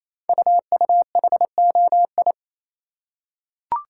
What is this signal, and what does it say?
Morse code, heard as a keyed tone, about 29 wpm, 700 Hz. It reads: UUHOI
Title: Code Groups - 5 Letters - 28wpm